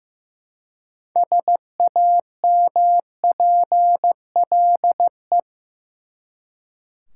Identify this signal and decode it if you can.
Morse code, heard as a keyed tone, about 15 wpm, 700 Hz. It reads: SAMPLE
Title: Six Letter Words - 15wpm